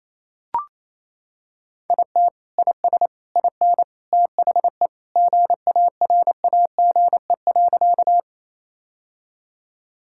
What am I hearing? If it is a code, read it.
Morse code, heard as a keyed tone, about 28 wpm, 700 Hz. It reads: IT IS IN THE GARAGE.
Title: Sentences from Top 1600 Words 28wpm